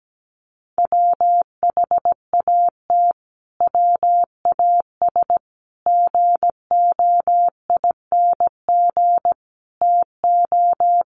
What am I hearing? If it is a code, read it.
Morse code, heard as a keyed tone, about 17 wpm, 700 Hz. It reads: WHAT WAS GOING TO
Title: Sets of 5 Words Encore 17wpm